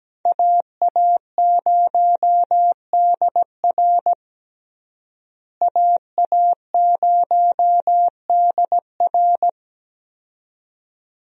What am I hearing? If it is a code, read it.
Morse code, heard as a keyed tone, about 17 wpm, 700 Hz. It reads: AA0DR AA0DR
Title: USA Call Signs - 5 Characters - Repeated 2x - 17wpm